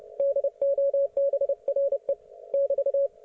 Morse code, heard as a keyed tone, about 29 wpm, 560 Hz. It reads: DOBRE <BT>